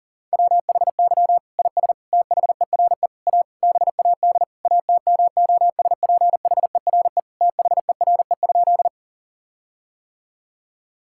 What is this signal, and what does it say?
Morse code, heard as a keyed tone, about 40 wpm, 700 Hz. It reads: WHY IS THERE A BAD ATMOSPHERE THERE?
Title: Sentences from Top 1600 Words 40wpm